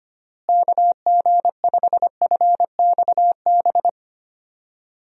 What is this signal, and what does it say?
Morse code, heard as a keyed tone, about 25 wpm, 700 Hz. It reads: KG5FXB
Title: USA Call Signs - 6 Characters - Repeated 3x - 25wpm